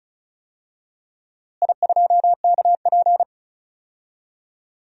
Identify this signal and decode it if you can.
Morse code, heard as a keyed tone, about 36 wpm, 700 Hz. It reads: I2KP
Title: DXCC-100 Call Signs - 4 Characters - Repeated 3x - 35wpm